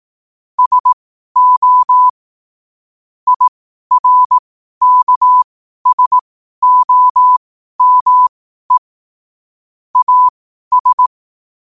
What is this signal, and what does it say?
Morse code, heard as a keyed tone, about 18 wpm, spaced as for 12 wpm, 1000 Hz. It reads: SO IRKSOME AS
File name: 12wpm